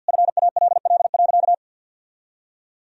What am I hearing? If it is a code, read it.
Morse code, heard as a keyed tone, about 46 wpm, 700 Hz. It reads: FALL.
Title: Sentences from Top 1000 Words - Review Part 2 - 50wpm